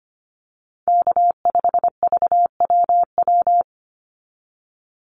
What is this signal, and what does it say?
Morse code, heard as a keyed tone, about 25 wpm, 700 Hz. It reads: K5VWW
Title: Call Signs - 25wpm